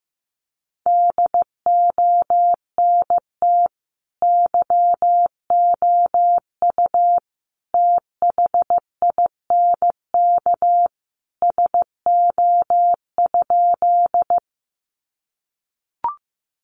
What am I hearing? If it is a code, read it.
Morse code, heard as a keyed tone, about 15 wpm, 700 Hz. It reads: DONT YOU THINK SO?
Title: Sentences from Top 300 Words 15wpm